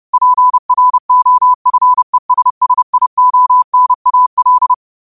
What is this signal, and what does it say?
Morse code, heard as a keyed tone, about 30 wpm, 1000 Hz. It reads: PROFESSIONAL